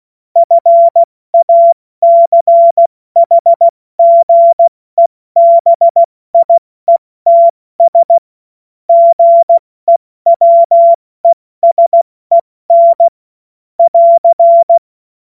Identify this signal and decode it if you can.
Morse code, heard as a keyed tone, about 16 wpm, 680 Hz. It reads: FACHGEBIETS GEWESEN <AR>
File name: Morsekurs / Witze / Witz 18. Juli 2023 70 F80